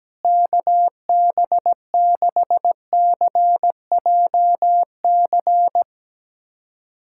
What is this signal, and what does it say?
Morse code, heard as a keyed tone, about 17 wpm, 700 Hz. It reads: KB6CJC
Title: USA Callsign DE Callsign - 6 Characters - 17wpm